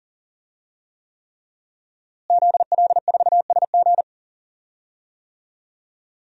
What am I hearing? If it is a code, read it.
Morse code, heard as a keyed tone, about 40 wpm, 700 Hz. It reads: ZL4SG
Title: DXCC-100 Call Signs - 5 Characters - Repeated 3x - 40wpm